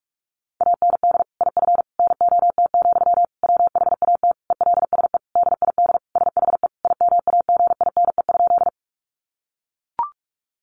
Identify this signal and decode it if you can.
Morse code, heard as a keyed tone, about 44 wpm, 700 Hz. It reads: AND IF NOT, WHAT ELSE DID SHE IMAGINE?
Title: Sentences from Top 1400 Words 45wpm